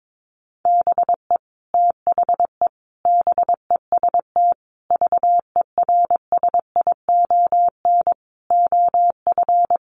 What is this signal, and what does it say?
Morse code, heard as a keyed tone, about 22 wpm, 700 Hz. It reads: BE THE BEST VERSION OF
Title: Sentences from Top 1400 Words 22wpm